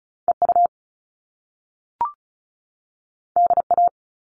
Morse code, heard as a keyed tone, about 36 wpm, 700 Hz. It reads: EU DA
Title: All Cut Numbers - 2-Digit 35wpm